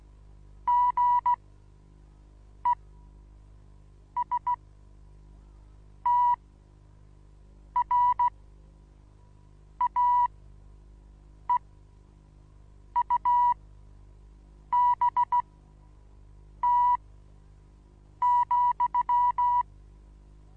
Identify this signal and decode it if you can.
Morse code, heard as a keyed tone, about 15 wpm, spaced as for 5 wpm, 980 Hz. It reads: GESTRAEUBT,
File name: Onlinelehrgang / morsen / DJ4UF Morsen25